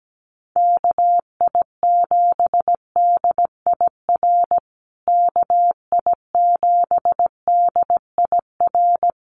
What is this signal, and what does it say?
Morse code, heard as a keyed tone, about 17 wpm, 700 Hz. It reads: KI7DIR KI7DIR
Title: USA Call Signs Repeated 3x 17wpm